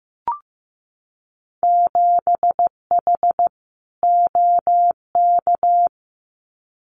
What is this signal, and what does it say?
Morse code, heard as a keyed tone, about 15 wpm, 700 Hz. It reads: 7H OK
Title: Winter Field Day Exchanges Encore 15wpm